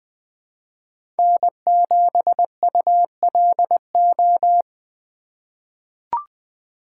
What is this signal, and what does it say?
Morse code, heard as a keyed tone, about 21 wpm, 700 Hz. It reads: N7ULO
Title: USA Call Signs - 5 Characters - 3x Character Spacing - 20wpm